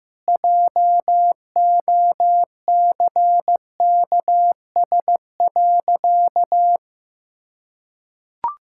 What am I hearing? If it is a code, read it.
Morse code, heard as a keyed tone, about 15 wpm, 700 Hz. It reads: JOCKS.
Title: Pangram Challenge v2 - 15wpm